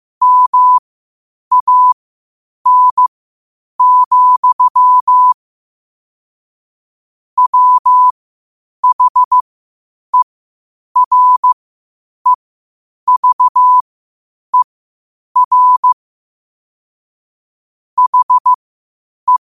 Morse code, heard as a keyed tone, about 15 wpm, spaced as for 8 wpm, 1000 Hz. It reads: MAN, WHEREVER HE
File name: qotdfiles / 8wpm